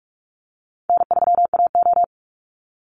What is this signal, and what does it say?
Morse code, heard as a keyed tone, about 44 wpm, 700 Hz. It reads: N3AO
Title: Easy Call Signs - 45wpm